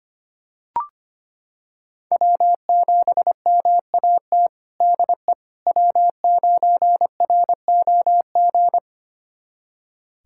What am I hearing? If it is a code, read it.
Morse code, heard as a keyed tone, about 25 wpm, 700 Hz. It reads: W7MAT DE W9ROG
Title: USA Callsign DE Callsign - 5 Characters - 25wpm